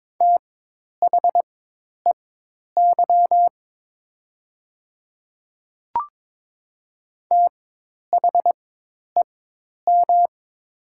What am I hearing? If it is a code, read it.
Morse code, heard as a keyed tone, about 22 wpm, spaced as for 10 wpm, 700 Hz. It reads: THEY THEM
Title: 4x Character Spacing - Top 100 Words - 22wpm